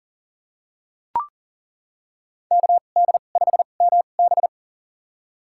Morse code, heard as a keyed tone, about 40 wpm, spaced as for 29 wpm, 700 Hz. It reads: KD5MB
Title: USA Call Signs - 5 Characters - 2x Character Spacing - 40wpm